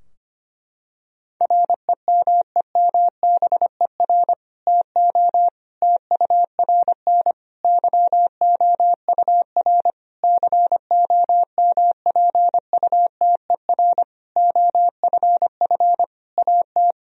Morse code, heard as a keyed tone, about 25 wpm, 700 Hz. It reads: REMEMBER TO TURN YOUR COMPUTER OFF AT
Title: Sentences from Top 1000 Words - Review Part 3 - 25wpm